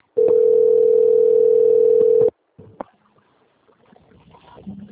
Ringback tone (North America), 440 + 480 Hz.